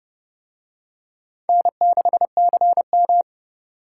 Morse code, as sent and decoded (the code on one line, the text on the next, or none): -. -.... -.-. --
N6CM